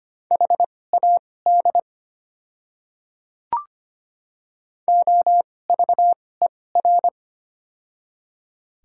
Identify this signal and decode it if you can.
Morse code, heard as a keyed tone, about 25 wpm, spaced as for 18 wpm, 700 Hz. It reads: HAD OVER
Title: 2x Character Spacing - Top 200 Words - 25wpm